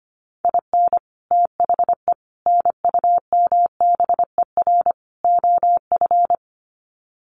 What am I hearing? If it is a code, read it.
Morse code, heard as a keyed tone, about 25 wpm, 700 Hz. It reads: IN THE NUMBER OF
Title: Sets of 4 Words 25wpm